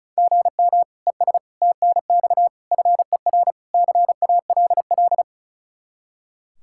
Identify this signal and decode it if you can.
Morse code, heard as a keyed tone, about 36 wpm, 700 Hz. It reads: GM ES TNX FER CALL
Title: Practice QSO - Exchange 1 - Beginning - 35wpm